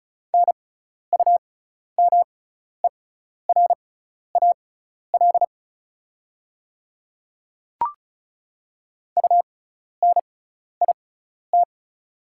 Morse code, heard as a keyed tone, about 34 wpm, spaced as for 12 wpm, 690 Hz. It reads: NUMERAL UNIT
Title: 6x Character Spacing - Top 500 Words - 35wpm